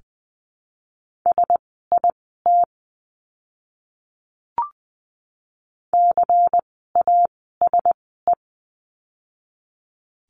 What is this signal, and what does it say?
Morse code, heard as a keyed tone, about 20 wpm, spaced as for 15 wpm, 700 Hz. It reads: SIT CASE
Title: 101 Lesson 17f - Letter u - Words 20wpm